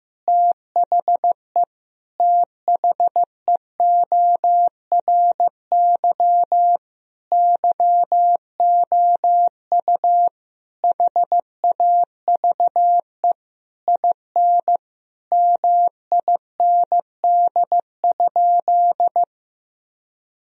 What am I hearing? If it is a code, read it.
Morse code, heard as a keyed tone, about 16 wpm, 700 Hz. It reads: THE THEORY YOU HAVE IN MIND?
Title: Sentences from Top 1000 Words - Review Part 3 - 15wpm